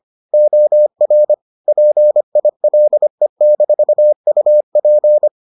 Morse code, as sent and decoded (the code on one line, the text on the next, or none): --- .-. / .--. .. .-.. -....- ..- .--.
OR PILE-UP